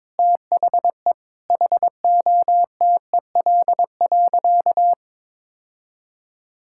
Morse code, heard as a keyed tone, about 21 wpm, 700 Hz. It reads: THE HOTEL.
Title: Sentences from Top 1600 Words 22wpm